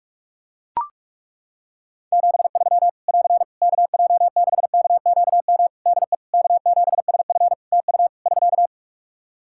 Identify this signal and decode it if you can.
Morse code, heard as a keyed tone, about 42 wpm, 700 Hz. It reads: 73 <AR> KJ6KQM DE K7SF TU <SK>
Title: Practice QSO - Exchange 4 - Ending - 45wpm